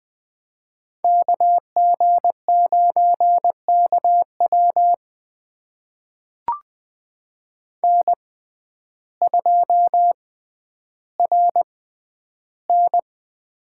Morse code, heard as a keyed tone, about 20 wpm, spaced as for 11 wpm, 700 Hz. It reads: KG9KW N2RN
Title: USA Call Signs - 6x Character Spacing - 20wpm